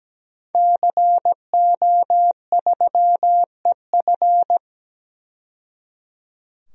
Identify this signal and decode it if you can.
Morse code, heard as a keyed tone, about 17 wpm, 700 Hz. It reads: CO3EF
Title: DXCC-100 Call Signs - 5 Characters - 17wpm